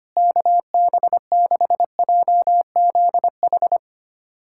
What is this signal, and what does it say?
Morse code, heard as a keyed tone, about 25 wpm, 700 Hz. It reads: KB6JZH
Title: USA Callsign DE Callsign - 6 Characters - 25wpm